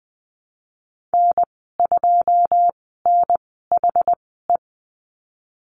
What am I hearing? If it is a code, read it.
Morse code, heard as a keyed tone, about 20 wpm, spaced as for 15 wpm, 700 Hz. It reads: N2NHE
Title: 122 Lesson 20g - Number 3 - Call Signs 20wpm